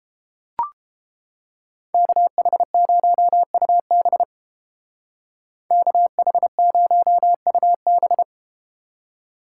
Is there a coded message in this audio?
Yes: Morse code from 1.9 to 8.2 s, heard as a keyed tone, about 30 wpm, 700 Hz. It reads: KH0UB KH0UB